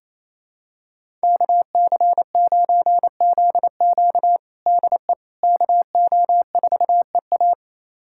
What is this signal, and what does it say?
Morse code, heard as a keyed tone, about 28 wpm, 700 Hz. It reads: KC9ZQ DE KO4EA
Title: USA Callsign DE Callsign - 5 Characters - 28wpm